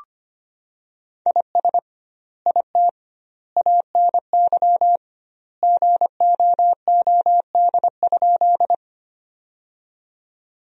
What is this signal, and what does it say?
Morse code, heard as a keyed tone, about 25 wpm, 700 Hz. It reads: IS IT ANY GOOD?
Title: 2x Word-Spacing - Sentences from Top 200 Words - 25wpm